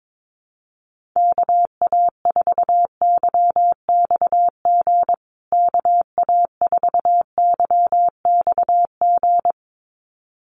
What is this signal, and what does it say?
Morse code, heard as a keyed tone, about 22 wpm, 700 Hz. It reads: KA4YXG KA4YXG
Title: USA Call Signs Repeated 2x 22wpm